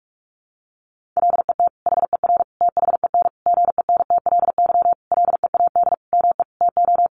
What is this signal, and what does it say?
Morse code, heard as a keyed tone, about 44 wpm, 700 Hz. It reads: LET HER THEN GENTLY LEAD ME TO
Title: Sentences from Top 1500 Words 45wpm